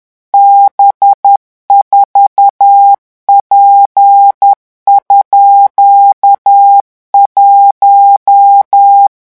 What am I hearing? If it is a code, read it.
Morse code, heard as a keyed tone, about 10 wpm, 800 Hz. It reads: B4P_1
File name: morse mp3 / revision17-vitesse 10 mots